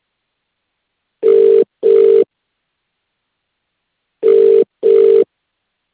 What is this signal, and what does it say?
Ringback tone (UK), 400 + 450 Hz.